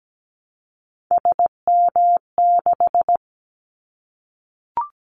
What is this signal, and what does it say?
Morse code, heard as a keyed tone, about 17 wpm, 700 Hz. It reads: SM6
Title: Callsign Prefixes for Top 25 DXCC - 17wpm